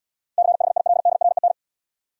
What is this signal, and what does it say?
Morse code, heard as a keyed tone, about 77 wpm, 700 Hz. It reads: CHILDREN